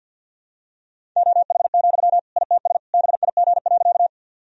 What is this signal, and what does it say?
Morse code, heard as a keyed tone, about 52 wpm, 700 Hz. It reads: OH, ITS BIG.
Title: Sentences from Top 1000 Words - Review Part 1 - 50wpm